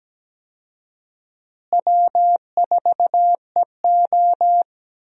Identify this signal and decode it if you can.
Morse code, heard as a keyed tone, about 17 wpm, 700 Hz. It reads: W4EO
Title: USA Call Signs - 4 Characters - Repeated 3x - 17wpm